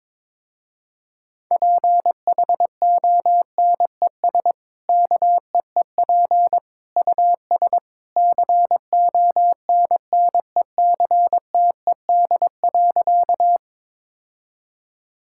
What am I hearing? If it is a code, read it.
Morse code, heard as a keyed tone, about 22 wpm, 700 Hz. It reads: PHONES KEEP US CONNECTED.
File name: practice / 22wpm / Easy Sentences with Hints - 22wpm